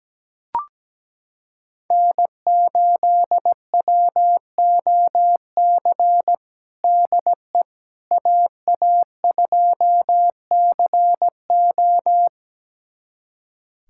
Morse code, heard as a keyed tone, about 17 wpm, 700 Hz. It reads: N8WOC DE AA2CO
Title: USA Callsign DE Callsign - 5 Characters - 17wpm